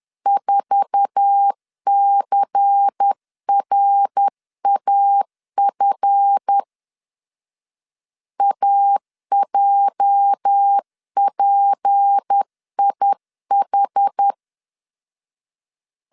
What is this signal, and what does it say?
Morse code, heard as a keyed tone, about 10 wpm, 800 Hz. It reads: 4CRAF AJPIH
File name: LEZIONI / 9 286 10wpm 2735